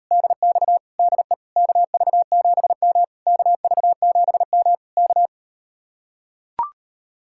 Morse code, heard as a keyed tone, about 39 wpm, 700 Hz. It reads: DX DE KV7M KV7M K